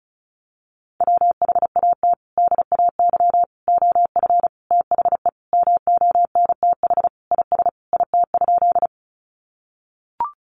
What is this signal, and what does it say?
Morse code, heard as a keyed tone, about 34 wpm, 700 Hz. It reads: WHAT DAY OF THE MONTH IS IT?
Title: Sentences from Top 800 Words 35wpm